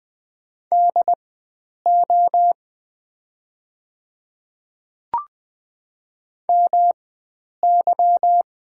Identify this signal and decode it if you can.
Morse code, heard as a keyed tone, about 20 wpm, spaced as for 9 wpm, 690 Hz. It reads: DO MY
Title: 4x Character Spacing - Top 100 Words - 20wpm